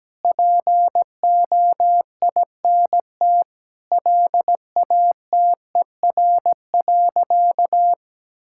Morse code, heard as a keyed tone, about 17 wpm, 700 Hz. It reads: POINT LATER.
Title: Sentences from Top 1000 Words - Review Part 6 - 17wpm